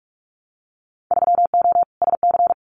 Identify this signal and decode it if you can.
Morse code, heard as a keyed tone, about 46 wpm, 700 Hz. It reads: 3O SC
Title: Winter Field Day Exchanges 45wpm